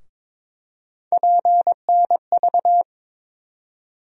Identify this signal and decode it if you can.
Morse code, heard as a keyed tone, about 23 wpm, 700 Hz. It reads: PNV
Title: USA Call Sign Postfixes - 3 Letters 22wpm